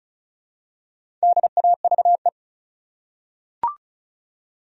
Morse code, heard as a keyed tone, about 36 wpm, 700 Hz. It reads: DAVE
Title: USA Common Names - Max 4 Letters - 35wpm